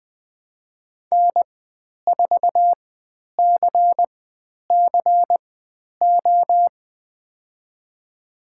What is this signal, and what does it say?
Morse code, heard as a keyed tone, about 20 wpm, spaced as for 10 wpm, 700 Hz. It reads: N4CCO
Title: 102 Lesson 17g - Letter u - Call Signs 20wpm